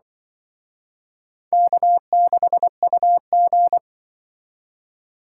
Morse code, heard as a keyed tone, about 25 wpm, 700 Hz. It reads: K6UG
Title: Speed Racing - USA 1x2 Call Signs 20wpm